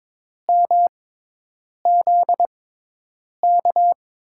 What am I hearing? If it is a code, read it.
Morse code, heard as a keyed tone, about 22 wpm, spaced as for 8 wpm, 700 Hz. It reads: MZK
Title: USA Call Signs - 5 Characters - 6x Character Spacing - 22wpm